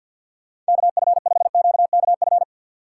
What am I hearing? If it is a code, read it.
Morse code, heard as a keyed tone, about 61 wpm, 700 Hz. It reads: KF5QKF
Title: Speed Racing - USA 2x3 Call Signs 50wpm